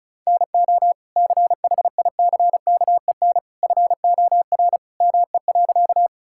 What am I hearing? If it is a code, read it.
Morse code, heard as a keyed tone, about 36 wpm, 700 Hz. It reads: NO CHICKEN FOR ME.
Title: Sentences from Top 1000 Words - Review Part 5 - 35wpm